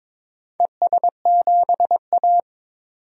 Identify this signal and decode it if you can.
Morse code, heard as a keyed tone, about 23 wpm, 690 Hz. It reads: ES7A
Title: DXCC-100 Call Signs - 4 Characters - 22wpm